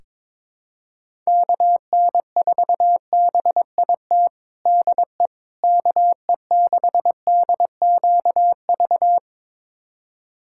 Morse code, heard as a keyed tone, about 22 wpm, 700 Hz. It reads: KN4BIT DE KE6DQV
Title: USA Callsign DE Callsign - 6 Characters - 22wpm